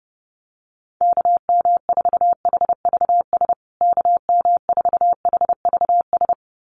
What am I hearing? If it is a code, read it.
Morse code, heard as a keyed tone, about 30 wpm, 700 Hz. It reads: KM4HVS KM4HVS
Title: USA Call Signs Repeated 2x 30wpm